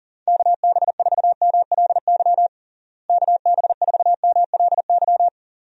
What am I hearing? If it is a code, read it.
Morse code, heard as a keyed tone, about 40 wpm, 700 Hz. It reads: KB4MLY KB4MLY
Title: USA Call Signs - 6 Characters - Repeated 2x - 40wpm